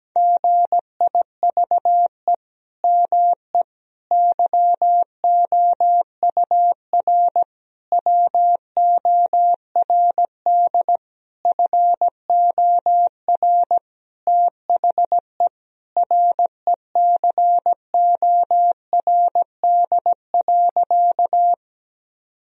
Morse code, heard as a keyed tone, about 17 wpm, 700 Hz. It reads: GIVE ME YOUR WORD FOR THE RECORD.
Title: Sentences from Top 1000 Words - Review Part 3 - 17wpm